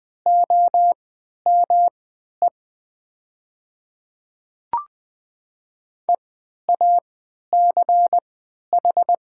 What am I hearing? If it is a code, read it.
Morse code, heard as a keyed tone, about 20 wpm, spaced as for 11 wpm, 690 Hz. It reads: OME EACH
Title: 3x Character Spacing - Top 200 Words - 20wpm